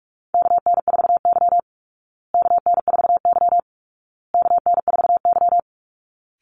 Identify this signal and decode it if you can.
Morse code, heard as a keyed tone, about 44 wpm, 700 Hz. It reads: KN4Y KN4Y KN4Y
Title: Call Signs Repeated 2x - 45wpm